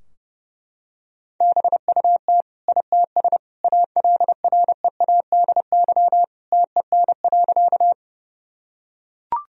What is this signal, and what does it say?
Morse code, heard as a keyed tone, about 28 wpm, 700 Hz. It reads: BUT ITS ALREADY TEN.
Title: Sentences from Top 1000 Words - Review Part 2 - 30wpm